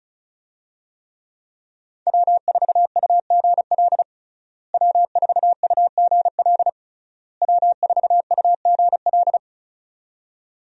Morse code, heard as a keyed tone, about 36 wpm, 700 Hz. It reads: W4UGL W4UGL W4UGL
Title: USA Call Signs - 5 Characters - Repeated 3x - 35wpm